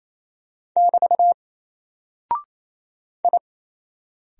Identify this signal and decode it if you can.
Morse code, heard as a keyed tone, about 28 wpm, 700 Hz. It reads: <BT> I
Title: Single Character (Letter:Number:Punctuation:Prosign) Alt 28wpm